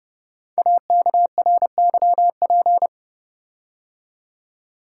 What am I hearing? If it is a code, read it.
Morse code, heard as a keyed tone, about 30 wpm, 700 Hz. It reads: AKRYP
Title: Code Groups - 5 Letters - 30wpm